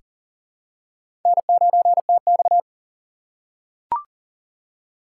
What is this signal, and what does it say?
Morse code, heard as a keyed tone, about 40 wpm, 700 Hz. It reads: N9TX
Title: USA Call Signs - 4x Character Spacing - 40wpm